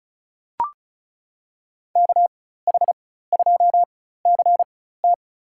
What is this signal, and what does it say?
Morse code, heard as a keyed tone, about 36 wpm, spaced as for 16 wpm, 700 Hz. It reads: KH2CT
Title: USA Call Signs - 4x Character Spacing - 35wpm